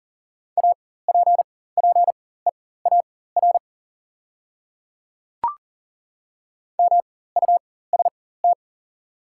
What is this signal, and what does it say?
Morse code, heard as a keyed tone, about 40 wpm, spaced as for 19 wpm, 690 Hz. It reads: APPEAR MUST
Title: 4x Character Spacing - Top 500 Words - 40wpm